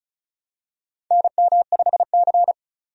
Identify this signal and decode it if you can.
Morse code, heard as a keyed tone, about 36 wpm, 700 Hz. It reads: NM5C
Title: USA Call Signs - 4 Characters - 35wpm